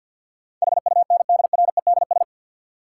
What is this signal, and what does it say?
Morse code, heard as a keyed tone, about 47 wpm, 700 Hz. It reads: HUNDREDS
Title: Sets of 4 Words - New Word Drill - 50wpm